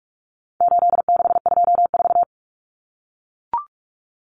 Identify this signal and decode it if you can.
Morse code, heard as a keyed tone, about 44 wpm, 700 Hz. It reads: 8624
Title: Common Cut Numbers - 4-Digit 45wpm